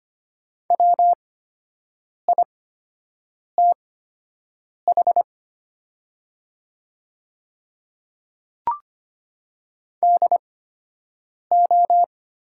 Morse code, heard as a keyed tone, about 25 wpm, spaced as for 7 wpm, 700 Hz. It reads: WITH DO
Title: 8x Character Spacing - Top 300 Words - 25wpm